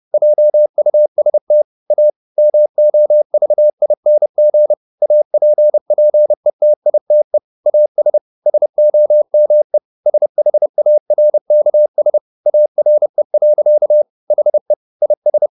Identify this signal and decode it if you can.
Morse code, heard as a keyed tone, about 30 wpm, 600 Hz. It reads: JUST A MOVING APPETITE AS SOME SHARKS ARE. HE IS